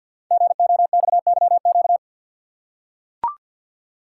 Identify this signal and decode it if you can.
Morse code, heard as a keyed tone, about 45 wpm, 700 Hz. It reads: GOXYQ
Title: Code Groups - 5 Letter:Numbers - 50wpm